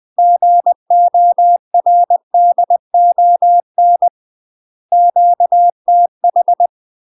Morse code, heard as a keyed tone, about 20 wpm, 700 Hz. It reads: GORDON QTH
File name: UK MASTER QSOs TRAINING / 20WPM / UK MASTER QSOs 0068